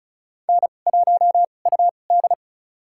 Morse code, heard as a keyed tone, about 34 wpm, spaced as for 25 wpm, 700 Hz. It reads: N1UD
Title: USA Call Signs - 4 Characters - 2x Character Spacing - 35wpm